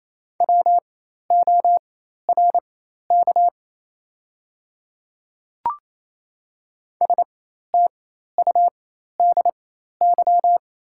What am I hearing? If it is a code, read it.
Morse code, heard as a keyed tone, about 28 wpm, spaced as for 13 wpm, 700 Hz. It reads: WORK STUDY
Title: 4x Character Spacing - Top 300 Words - 28wpm